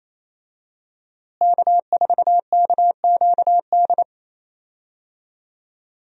Morse code, heard as a keyed tone, about 27 wpm, 700 Hz. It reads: K4KQD
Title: USA Call Signs - 5 Characters - Repeated 3x - 28wpm